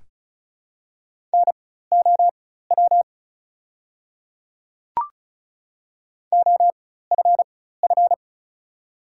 Morse code, heard as a keyed tone, about 36 wpm, spaced as for 16 wpm, 700 Hz. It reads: NOW OFF
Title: 4x Character Spacing - Top 200 Words - 35wpm